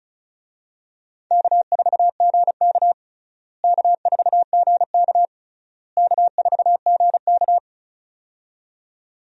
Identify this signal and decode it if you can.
Morse code, heard as a keyed tone, about 34 wpm, 700 Hz. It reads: K4GK K4GK K4GK
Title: USA Call Signs - 4 Characters - Repeated 3x - 35wpm